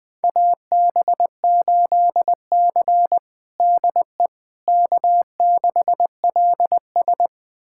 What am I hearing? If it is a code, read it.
Morse code, heard as a keyed tone, about 20 wpm, 700 Hz. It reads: AB8C DE K6LS
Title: USA Callsign DE Callsign - 4 Characters - 20wpm